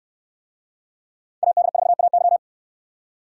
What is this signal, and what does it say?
Morse code, heard as a keyed tone, about 69 wpm, 700 Hz. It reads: AD4AY